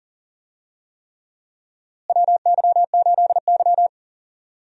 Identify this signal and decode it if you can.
Morse code, heard as a keyed tone, about 40 wpm, 700 Hz. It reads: WY8Y
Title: USA Call Signs - 4 Characters - Repeated 3x - 40wpm